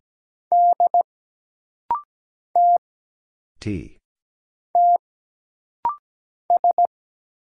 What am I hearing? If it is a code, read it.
Morse code, heard as a keyed tone, about 17 wpm, 700 Hz. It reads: D T T S
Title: Single Letter-Number Alt - Rapid-Fire 17wpm